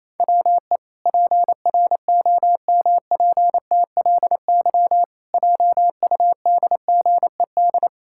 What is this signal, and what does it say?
Morse code, heard as a keyed tone, about 28 wpm, 700 Hz. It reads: WE PROMPTLY JUDGED
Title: Pangram Challenge v2 - 28wpm